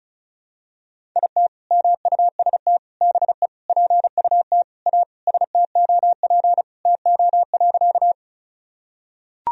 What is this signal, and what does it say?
Morse code, heard as a keyed tone, about 34 wpm, 700 Hz. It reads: IT MUST BE PUT A STOP TO.
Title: Sentences from Top 1000 Words - Review Part 1 - 35wpm